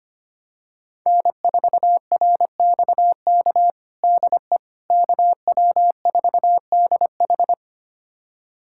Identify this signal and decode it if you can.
Morse code, heard as a keyed tone, about 25 wpm, 700 Hz. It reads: N4RXK DE KW4DH
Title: USA Callsign DE Callsign - 5 Characters - 25wpm